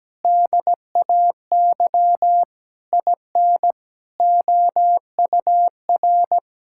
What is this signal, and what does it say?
Morse code, heard as a keyed tone, about 17 wpm, 700 Hz. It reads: DAY IN OUR
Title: Sentences from Top 1600 Words 17wpm